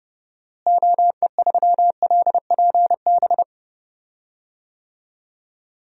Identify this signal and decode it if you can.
Morse code, heard as a keyed tone, about 30 wpm, 700 Hz. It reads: OE3LPB
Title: DXCC-100 Call Signs - 6 Characters - Repeated 2x - 30wpm